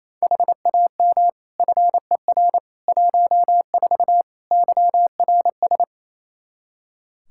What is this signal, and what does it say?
Morse code, heard as a keyed tone, about 28 wpm, 700 Hz. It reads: HAM FER 14 YRS
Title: K1USN SST QSO Practice 28wpm